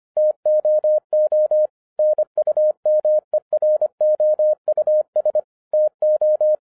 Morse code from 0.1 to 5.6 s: TOO NUMEROUS